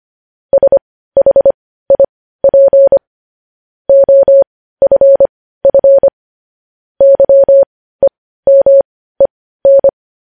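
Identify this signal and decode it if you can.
Morse code, heard as a keyed tone, about 25 wpm, spaced as for 15 wpm, 550 Hz. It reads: SHIP OFF YEMEN
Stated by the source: Morse code news headlines practice at 15, preceded by vvv.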